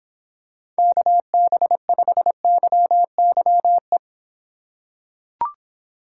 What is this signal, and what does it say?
Morse code, heard as a keyed tone, about 25 wpm, 700 Hz. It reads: KB5YYE